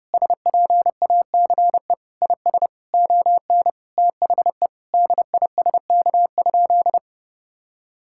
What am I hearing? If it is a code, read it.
Morse code, heard as a keyed tone, about 32 wpm, 700 Hz. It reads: SPACE IS ON THE DISK?
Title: Sentences from Top 1000 Words - Review Part 6 - 30wpm